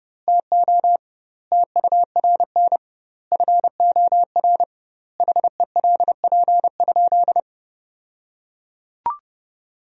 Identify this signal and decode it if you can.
Morse code, heard as a keyed tone, about 32 wpm, 700 Hz. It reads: TO TURN FOR HELP?
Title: 2x Word-Spacing - Sentences from Top 200 Words - 30wpm